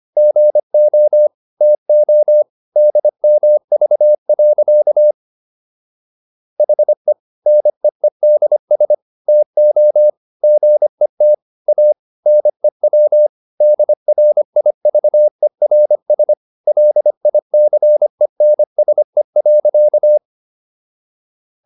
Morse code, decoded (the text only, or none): GO TO DMV. HE NEEDS TO GET A NEW DRIVERS LICENSE.